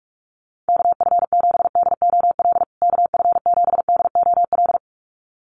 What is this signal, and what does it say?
Morse code, heard as a keyed tone, about 44 wpm, 700 Hz. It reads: KF7DOL KF7DOL
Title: USA Call Signs Repeated 2x 45wpm